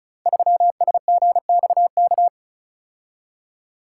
Morse code, heard as a keyed tone, about 36 wpm, 700 Hz. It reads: 3SGXK
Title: Code Groups - 5 Letter:Numbers - 35wpm